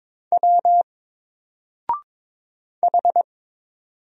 Morse code, heard as a keyed tone, about 22 wpm, 700 Hz. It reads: W H
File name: Single-Letter-Number-22wpm